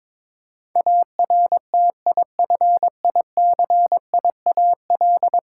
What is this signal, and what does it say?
Morse code, heard as a keyed tone, about 22 wpm, 700 Hz. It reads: ARTIFICIAL
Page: morse-code-ninja-practice-22wpm